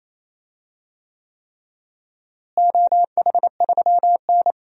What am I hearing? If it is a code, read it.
Morse code, heard as a keyed tone, about 28 wpm, 700 Hz. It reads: OH3N
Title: DXCC-100 Call Signs - 4 Characters - Repeated 3x - 28wpm